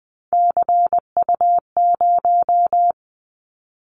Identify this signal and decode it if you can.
Morse code, heard as a keyed tone, about 20 wpm, 700 Hz. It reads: CU0
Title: Callsign Prefixes for Top 100 DXCC - 20wpm